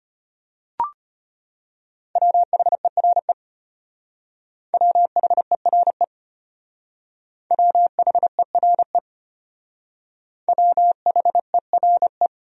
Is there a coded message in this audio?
Yes: Morse code from 2.2 to 12.3 s, heard as a keyed tone, about 32 wpm, 700 Hz. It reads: WHERE WHERE WHERE WHERE